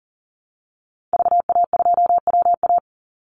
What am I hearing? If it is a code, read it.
Morse code, heard as a keyed tone, about 40 wpm, 700 Hz. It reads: VA2WA
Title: Call Signs Repeated 2x - 40wpm